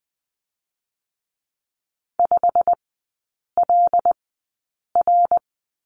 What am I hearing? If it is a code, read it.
Morse code, heard as a keyed tone, about 21 wpm, spaced as for 8 wpm, 700 Hz. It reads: 5LR
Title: 087 Lesson 15e - Number 5 - Three Characters 20wpm